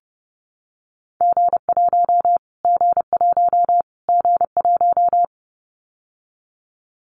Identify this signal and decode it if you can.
Morse code, heard as a keyed tone, about 30 wpm, 690 Hz. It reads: G1 G1 G1
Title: Callsign Prefixes for Top 25 DXCC - Repeated 3x - 30wpm